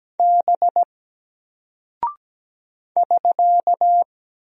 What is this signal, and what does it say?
Morse code, heard as a keyed tone, about 17 wpm, 700 Hz. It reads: B <SK>
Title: Single Character (Letter:Number:Punctuation:Prosign) Alt 17wpm